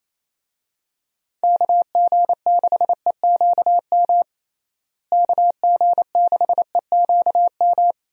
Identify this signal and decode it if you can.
Morse code, heard as a keyed tone, about 28 wpm, 700 Hz. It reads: KG6EQM KG6EQM
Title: USA Call Signs - 6 Characters - Repeated 3x - 28wpm